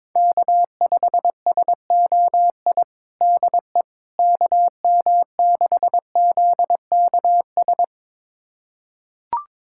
Morse code, heard as a keyed tone, about 22 wpm, 700 Hz. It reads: K5SOI DE KM6ZKS
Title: USA Callsign DE Callsign 22wpm